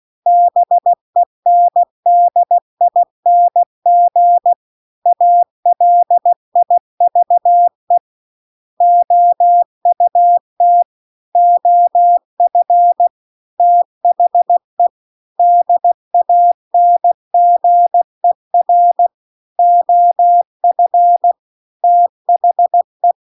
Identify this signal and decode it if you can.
Morse code, heard as a keyed tone, about 16 wpm, 700 Hz. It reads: BENDING ALIVE OUT OF THE DANGER OF THE
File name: CW-The-Big-Twoheaded-River-2-16WPM